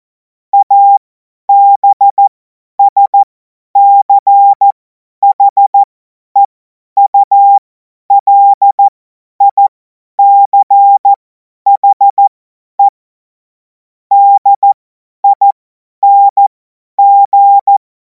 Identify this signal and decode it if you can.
Morse code, heard as a keyed tone, about 14 wpm, spaced as for 10 wpm, 800 Hz. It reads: ABSCHEULICHE DING
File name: Morsekurs / Witze / Witz 17. März 2026 50 F70